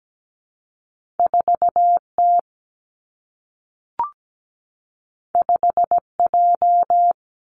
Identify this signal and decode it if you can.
Morse code, heard as a keyed tone, about 17 wpm, 700 Hz. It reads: 4T 5J
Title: Two Challenging Letter-Number Combinations 17wpm